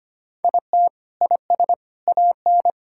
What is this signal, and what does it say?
Morse code, heard as a keyed tone, about 25 wpm, 700 Hz. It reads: IT IS AN
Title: Sentences from Top 1000 Words - Review Part 6 - 25wpm